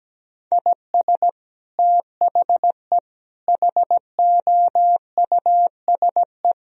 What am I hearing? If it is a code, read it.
Morse code, heard as a keyed tone, about 17 wpm, 700 Hz. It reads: IS THE HOUSE
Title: Sentences from Top 1000 Words - Review Part 1 - 17wpm